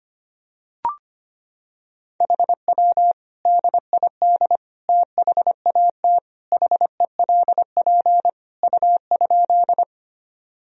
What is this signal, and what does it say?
Morse code, heard as a keyed tone, about 25 wpm, 700 Hz. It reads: HW DID THAT HELP U?
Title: Sentences from Top 200 Words and Abbreviations 25wpm